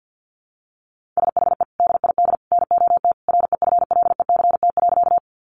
Morse code, heard as a keyed tone, about 50 wpm, 700 Hz. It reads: SHE DID NOT REFLECT.
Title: Sentences from Top 1500 Words 50wpm